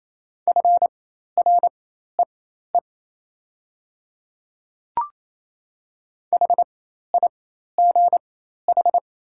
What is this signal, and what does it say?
Morse code, heard as a keyed tone, about 28 wpm, spaced as for 13 wpm, 700 Hz. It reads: FREE HIGH
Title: 4x Character Spacing - Top 500 Words - 28wpm